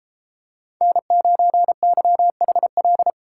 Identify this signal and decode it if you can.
Morse code, heard as a keyed tone, about 32 wpm, 700 Hz. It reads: N9YHL